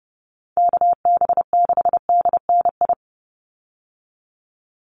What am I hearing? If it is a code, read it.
Morse code, heard as a keyed tone, about 30 wpm, 700 Hz. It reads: KB6DNI
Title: USA Call Signs 30wpm